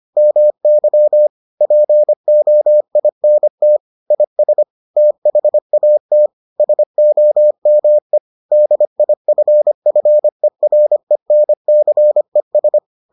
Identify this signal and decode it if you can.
Morse code, heard as a keyed tone, about 25 wpm, 600 Hz. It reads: MY POINT IS THAT SOME DIFFERENCES